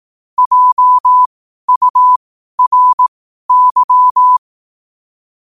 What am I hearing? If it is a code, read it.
Morse code, heard as a keyed tone, about 18 wpm, spaced as for 12 wpm, 1000 Hz. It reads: JURY
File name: qotdfiles / 12wpm